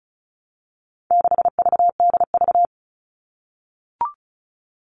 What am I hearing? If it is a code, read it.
Morse code, heard as a keyed tone, about 34 wpm, 700 Hz. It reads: 6VDV
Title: All Cut Numbers - 4-Digit 35wpm